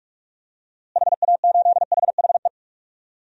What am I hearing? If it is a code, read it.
Morse code, heard as a keyed tone, about 46 wpm, 700 Hz. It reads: HA8HHE
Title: DXCC-100 Call Signs - 6 Characters - Repeated 2x - 45wpm